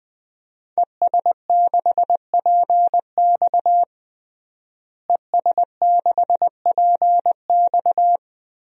Morse code, heard as a keyed tone, about 20 wpm, 700 Hz. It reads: ES6PX ES6PX
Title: DXCC-100 Call Signs - 5 Characters - Repeated 3x - 20wpm